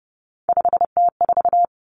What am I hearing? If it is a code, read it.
Morse code, heard as a keyed tone, about 30 wpm, 700 Hz. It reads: 5T4
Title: Common Cut Numbers - 3-Digit 30wpm